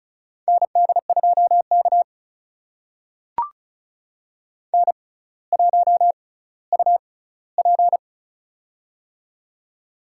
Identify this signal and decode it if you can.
Morse code, heard as a keyed tone, about 34 wpm, spaced as for 18 wpm, 700 Hz. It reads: ND2K N1UP
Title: USA Call Signs - 4 Characters - 6x Character Spacing - 35wpm